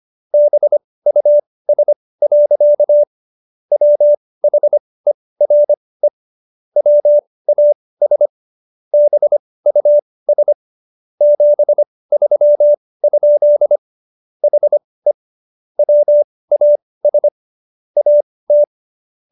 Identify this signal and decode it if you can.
Morse code, heard as a keyed tone, about 25 wpm, spaced as for 18 wpm, 590 Hz. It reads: BUS. WHERE WAS BUS 73? HE WAS AT